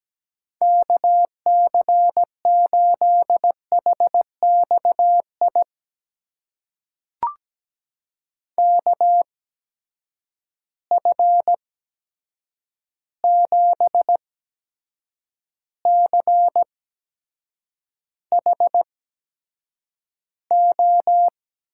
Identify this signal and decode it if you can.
Morse code, heard as a keyed tone, about 17 wpm, spaced as for 7 wpm, 700 Hz. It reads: KC8HXI KF7CHO
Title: USA Call Signs - 6 Characters - 8x Character Spacing - 17wpm